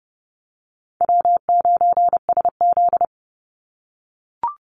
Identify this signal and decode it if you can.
Morse code, heard as a keyed tone, about 30 wpm, 700 Hz. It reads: W9SZ
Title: Easy Call Signs - 30wpm